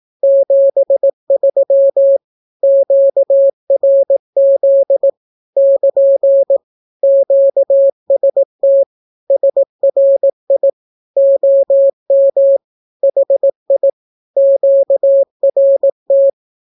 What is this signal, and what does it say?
Morse code, heard as a keyed tone, about 18 wpm, 550 Hz. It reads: 73 QRZ <KN> QST SRI OM HI QRT
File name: naukapodstaw / skroty / skroty